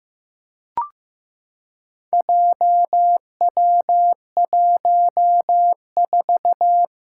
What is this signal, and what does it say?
Morse code, heard as a keyed tone, about 15 wpm, 700 Hz. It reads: JW14
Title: Four Challenging Letter-Number Combinations 15wpm